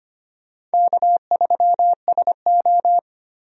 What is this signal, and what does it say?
Morse code, heard as a keyed tone, about 25 wpm, 700 Hz. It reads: K3SO
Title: USA Call Signs - 4 Characters - Repeated 3x - 25wpm